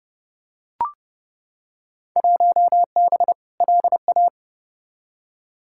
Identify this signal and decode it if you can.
Morse code, heard as a keyed tone, about 31 wpm, 700 Hz. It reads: 1B LA
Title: Field Day Exchanges Encore - 30wpm